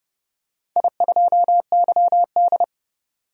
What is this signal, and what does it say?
Morse code, heard as a keyed tone, about 30 wpm, 700 Hz. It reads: I2YD
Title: DXCC-100 Call Signs - 4 Characters - 30wpm